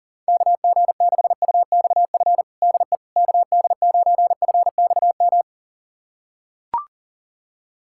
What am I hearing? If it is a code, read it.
Morse code, heard as a keyed tone, about 40 wpm, 700 Hz. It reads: KG6UXF DE KD9FXM
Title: USA Callsign DE Callsign 40wpm